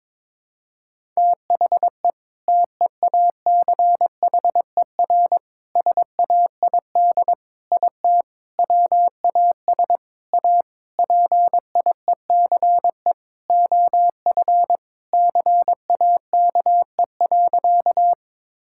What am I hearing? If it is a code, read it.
Morse code, heard as a keyed tone, about 21 wpm, 700 Hz. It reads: THE TEACHER SAID IT WAS A PIECE OF CAKE.
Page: morse-code-ninja-practice-22wpm